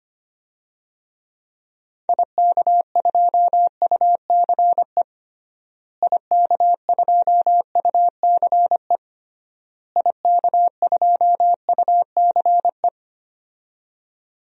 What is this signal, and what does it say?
Morse code, heard as a keyed tone, about 25 wpm, 700 Hz. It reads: IK2UCE IK2UCE IK2UCE
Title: DXCC-100 Call Signs - 6 Characters - Repeated 3x - 25wpm